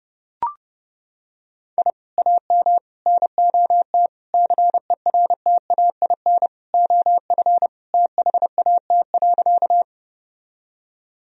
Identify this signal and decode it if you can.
Morse code, heard as a keyed tone, about 32 wpm, 700 Hz. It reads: I AM NOT CERTAIN OF THAT.
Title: Sentences from Top 1000 Words - Review Part 2 - 30wpm